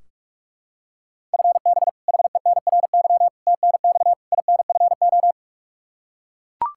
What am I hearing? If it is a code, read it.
Morse code, heard as a keyed tone, about 41 wpm, 700 Hz. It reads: FB HENRY TNX INFO
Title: Practice QSO - Exchange 3 - More Info - 45wpm